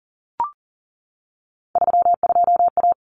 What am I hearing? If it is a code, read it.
Morse code, heard as a keyed tone, about 40 wpm, 700 Hz. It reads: 32A
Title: Common Cut Numbers - 3-Digit 40wpm